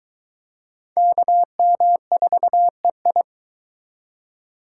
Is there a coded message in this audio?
Yes: Morse code from 1.0 to 3.2 s, heard as a keyed tone, about 22 wpm, 700 Hz. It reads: KM4EI